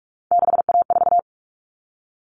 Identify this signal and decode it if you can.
Morse code, heard as a keyed tone, about 44 wpm, 700 Hz. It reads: 6A4
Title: All Cut Numbers - 3-Digit 45wpm